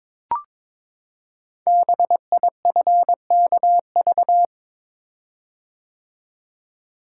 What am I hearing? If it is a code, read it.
Morse code, heard as a keyed tone, about 22 wpm, 700 Hz. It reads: BIFKV
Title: Code Groups - 5 Letter:Numbers - 22wpm